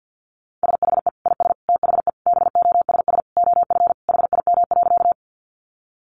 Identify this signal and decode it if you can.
Morse code, heard as a keyed tone, about 50 wpm, 700 Hz. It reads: SHE IS THE BOSS OF HIM.
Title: Sentences from Top 700 Words 50wpm